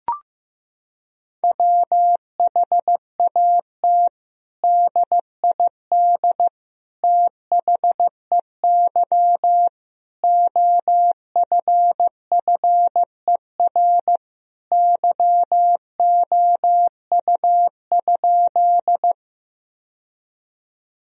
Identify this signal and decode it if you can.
Morse code, heard as a keyed tone, about 15 wpm, 700 Hz. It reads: WHAT DID THEY OFFER YOU?
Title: Sentences from Top 1000 Words - Review Part 4 - 15wpm